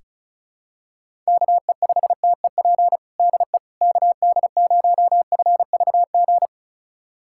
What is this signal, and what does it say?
Morse code, heard as a keyed tone, about 36 wpm, 700 Hz. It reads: KE5TEP DE KD0FVG
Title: USA Callsign DE Callsign - 6 Characters - 35wpm